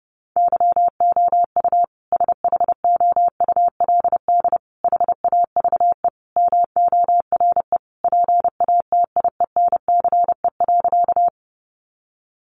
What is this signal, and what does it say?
Morse code, heard as a keyed tone, about 30 wpm, 700 Hz. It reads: YOU SHOULD HAVE MORE PATIENCE.
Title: Sentences from Top 1300 Words 30wpm